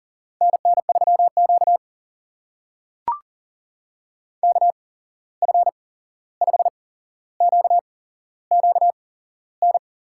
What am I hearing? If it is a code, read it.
Morse code, heard as a keyed tone, about 42 wpm, spaced as for 15 wpm, 700 Hz. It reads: NN3Q KF5QQN
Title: USA Call Signs - 8x Character Spacing - 40wpm